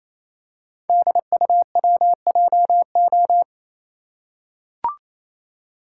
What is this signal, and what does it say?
Morse code, heard as a keyed tone, about 27 wpm, 700 Hz. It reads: DUWJO
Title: Code Groups - 5 Letter:Numbers - 28wpm